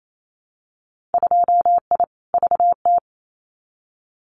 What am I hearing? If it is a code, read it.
Morse code, heard as a keyed tone, about 28 wpm, 700 Hz. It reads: 2I VT
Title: Winter Field Day Exchanges 28wpm